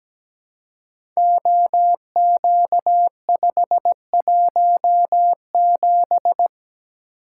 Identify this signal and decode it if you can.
Morse code, heard as a keyed tone, about 17 wpm, 690 Hz. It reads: OQ517
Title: Code Groups - 5 Letter:Numbers - 17wpm